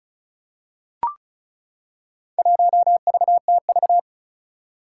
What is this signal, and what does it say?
Morse code, heard as a keyed tone, about 36 wpm, 700 Hz. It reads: 1VTV
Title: Four Challenging Letter-Number Combinations 35wpm